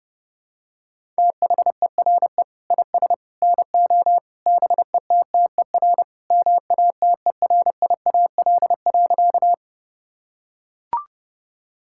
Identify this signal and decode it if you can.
Morse code, heard as a keyed tone, about 32 wpm, 700 Hz. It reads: THERE IS NO BETTER MATERIAL.
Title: Sentences from Top 1000 Words - Review Part 5 - 30wpm